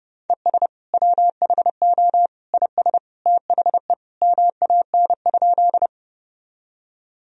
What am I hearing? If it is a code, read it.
Morse code, heard as a keyed tone, about 30 wpm, 700 Hz. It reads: ES WHO IS THE MAN?
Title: Sentences from Top 200 Words and Abbreviations 30wpm